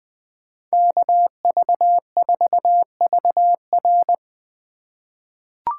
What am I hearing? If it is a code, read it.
Morse code, heard as a keyed tone, about 20 wpm, 700 Hz. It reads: KV4VR
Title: USA Call Signs - 5 Characters - 4x Character Spacing - 20wpm